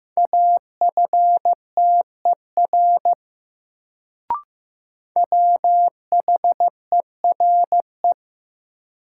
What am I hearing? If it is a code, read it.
Morse code, heard as a keyed tone, about 15 wpm, 700 Hz. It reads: AFTER WHERE
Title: Sentences from Top 200 Words - New Word Drill - 15wpm